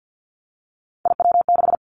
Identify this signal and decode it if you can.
Morse code, heard as a keyed tone, about 50 wpm, 700 Hz. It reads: IW6
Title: Three Challenging Letter-Number Combinations 50wpm